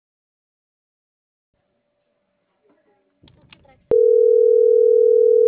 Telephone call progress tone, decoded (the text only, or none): ringback tone